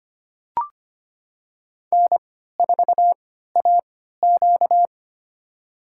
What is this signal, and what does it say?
Morse code, heard as a keyed tone, about 25 wpm, spaced as for 14 wpm, 700 Hz. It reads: N4AQ
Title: USA Call Signs - 4 Characters - 3x Character Spacing - 25wpm